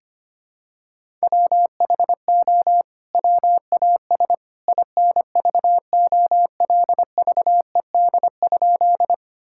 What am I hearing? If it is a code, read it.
Morse code, heard as a keyed tone, about 25 wpm, 700 Hz. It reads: WHO WAS INVOLVED?
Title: Sentences from Top 1000 Words - Review Part 5 - 25wpm